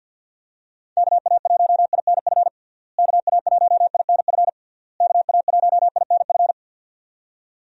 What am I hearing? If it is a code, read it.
Morse code, heard as a keyed tone, about 52 wpm, 700 Hz. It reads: KA1INF KA1INF KA1INF
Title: USA Call Signs - 6 Characters - Repeated 3x - 50wpm